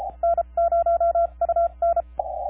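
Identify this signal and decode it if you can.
Morse code, heard as a keyed tone, about 33 wpm, spaced as for 28 wpm, 1360 Hz, with noise. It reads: N0UN
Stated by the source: KP5/NP3VI on Desecheo heard in a 15 meter CW pileup.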